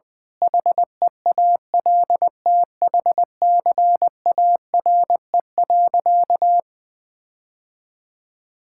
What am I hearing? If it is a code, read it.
Morse code, heard as a keyed tone, about 20 wpm, 700 Hz. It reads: HEALTHCARE.
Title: Sentences with 7-Word Phrases - 20wpm